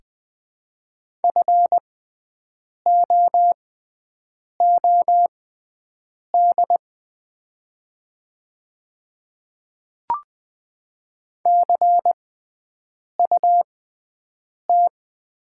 Morse code, heard as a keyed tone, about 21 wpm, spaced as for 7 wpm, 700 Hz. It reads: FOOD CUT
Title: 6x Character Spacing - Top 500 Words - 20wpm